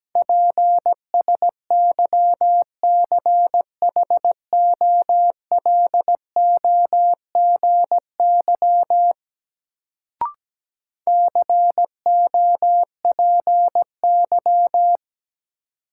Morse code, heard as a keyed tone, about 17 wpm, 700 Hz. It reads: PSYCHOLOGY COPY
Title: Sentences from Top 1400 Words - New Word Drill - 17wpm